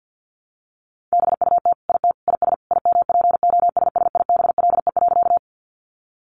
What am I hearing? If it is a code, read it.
Morse code, heard as a keyed tone, about 50 wpm, 700 Hz. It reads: BUT IT IS IMPOSSIBLE.
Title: Sentences from Top 1100 Words 50wpm